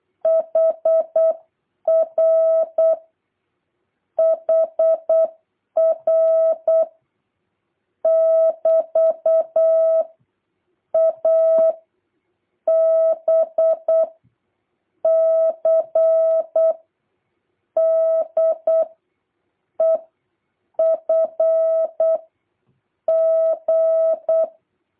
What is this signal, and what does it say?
Morse code, heard as a keyed tone, about 8 wpm, 650 Hz. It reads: HR HR <BT> A B C D E F G